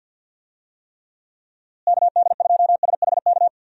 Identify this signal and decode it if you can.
Morse code, heard as a keyed tone, about 51 wpm, 700 Hz. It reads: KD2SHK
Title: USA Call Signs - 6 Characters - Repeated 3x - 50wpm